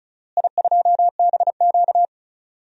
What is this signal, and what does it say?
Morse code, heard as a keyed tone, about 36 wpm, 700 Hz. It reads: I2BQ
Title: DXCC-100 Call Signs - 4 Characters - Repeated 3x - 35wpm